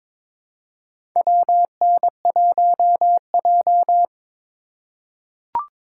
Morse code, heard as a keyed tone, about 22 wpm, 700 Hz. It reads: WN1J
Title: USA Call Signs - 4 Characters - 2x Character Spacing - 22wpm